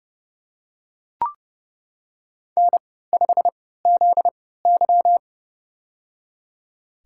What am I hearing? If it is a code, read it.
Morse code, heard as a keyed tone, about 30 wpm, spaced as for 17 wpm, 700 Hz. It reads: N5ZY
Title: USA Call Signs - 3x Character Spacing - 30wpm